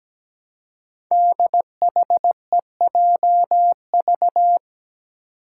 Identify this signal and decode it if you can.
Morse code, heard as a keyed tone, about 17 wpm, 700 Hz. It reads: DHEJV
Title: Five Challenging Letter-Number Combinations 17wpm